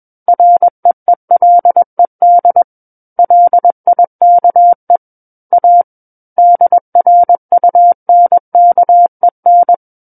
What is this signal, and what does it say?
Morse code, heard as a keyed tone, about 21 wpm, 700 Hz. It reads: REELED LIKE A DRUNKEN
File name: Book One / 12-Chapter 12 - 21 WPM